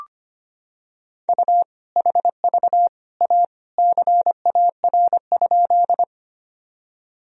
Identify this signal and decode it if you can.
Morse code, heard as a keyed tone, about 25 wpm, 700 Hz. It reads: U HV A CAR?
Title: Sentences from Top 300 Words and Abbreviations 25wpm